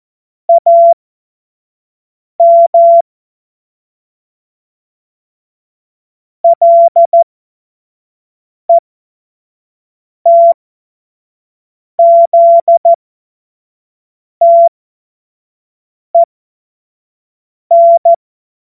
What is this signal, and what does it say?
Morse code, heard as a keyed tone, about 14 wpm, spaced as for 5 wpm, 680 Hz. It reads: AM LETZTEN
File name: Morsekurs / Witze / Witz 14. Februar 2023 25 F70